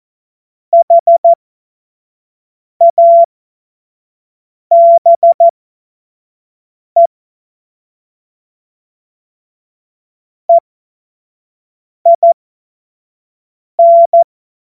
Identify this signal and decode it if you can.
Morse code, heard as a keyed tone, about 14 wpm, spaced as for 5 wpm, 680 Hz. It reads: HABE EIN
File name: Morsekurs / Witze / Witz 23 April 2024 25 F70